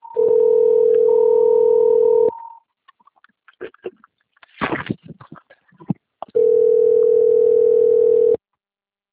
Ringback tone (North America), 440 + 480 Hz.